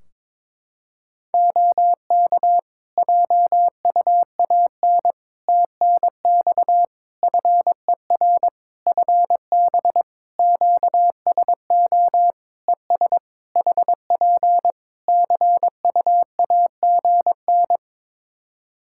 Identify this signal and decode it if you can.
Morse code, heard as a keyed tone, about 22 wpm, 700 Hz. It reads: OK JUAN TNX FER FB QSO ES HP CUAGN
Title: Practice QSO - Exchange 4 - Ending - 22wpm